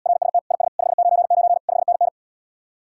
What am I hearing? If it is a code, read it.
Morse code, heard as a keyed tone, about 80 wpm, 700 Hz. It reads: RST IS 599 5NN